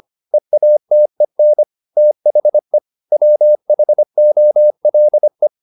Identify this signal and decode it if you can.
Morse code, heard as a keyed tone, about 25 wpm, 600 Hz. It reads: EATEN THE WHOLE